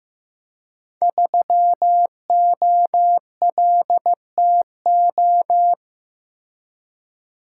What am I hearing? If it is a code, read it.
Morse code, heard as a keyed tone, about 15 wpm, 700 Hz. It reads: 3OLTO
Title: Code Groups - 5 Letter:Numbers - 15wpm